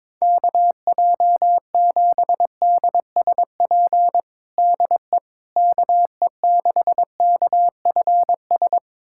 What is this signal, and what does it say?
Morse code, heard as a keyed tone, about 22 wpm, 700 Hz. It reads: KJ7DSP DE KE6KFS
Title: USA Callsign DE Callsign - 6 Characters - 22wpm